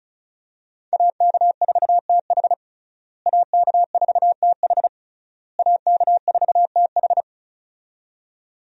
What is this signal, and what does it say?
Morse code, heard as a keyed tone, about 36 wpm, 700 Hz. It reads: AK4TH AK4TH AK4TH
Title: USA Call Signs - 5 Characters - Repeated 3x - 35wpm